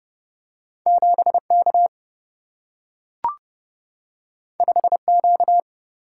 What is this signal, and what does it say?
Morse code, heard as a keyed tone, about 30 wpm, 700 Hz. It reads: 7 K 5 Q
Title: Single-Letter Single-Number 30wpm